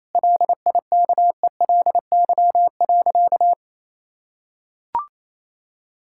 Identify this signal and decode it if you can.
Morse code, heard as a keyed tone, about 28 wpm, 700 Hz. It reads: LIKELY.
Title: Sentences from Top 1600 Words 28wpm